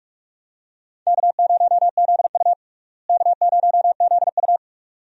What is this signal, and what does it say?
Morse code, heard as a keyed tone, about 47 wpm, 700 Hz. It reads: K0ZU K0ZU
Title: USA Call Signs - 4 Characters - Repeated 2x - 45wpm